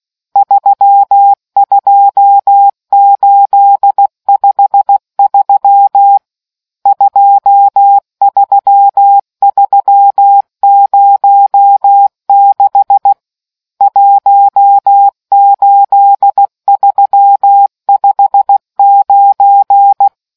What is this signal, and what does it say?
Morse code, heard as a keyed tone, about 16 wpm, 800 Hz. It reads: 32853 23306 18359